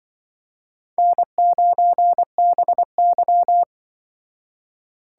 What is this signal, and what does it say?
Morse code, heard as a keyed tone, about 24 wpm, 700 Hz. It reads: N9BY
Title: Speed Racing - USA 1x2 Call Signs 20wpm